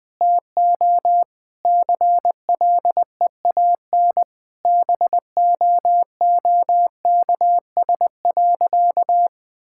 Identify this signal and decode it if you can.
Morse code, heard as a keyed tone, about 20 wpm, 700 Hz. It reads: TO CLEAN BOOKS.
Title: Sentences with 4-Word Phrases - 20wpm